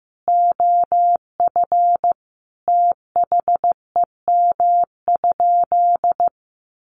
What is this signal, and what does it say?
Morse code, heard as a keyed tone, about 14 wpm, 700 Hz. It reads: OF THEM?
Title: Sentences from Top 900 Words 15wpm